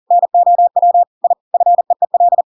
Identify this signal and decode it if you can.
Morse code, heard as a keyed tone, about 40 wpm, 700 Hz. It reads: NOW I FEEL
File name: Pinocchio / Pinocchio ch7 40 40 ik1hge